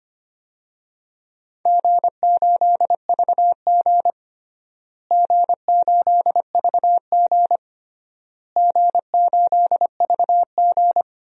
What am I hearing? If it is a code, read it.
Morse code, heard as a keyed tone, about 25 wpm, 700 Hz. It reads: G8VG G8VG G8VG
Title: DXCC-100 Call Signs - 4 Characters - Repeated 3x - 25wpm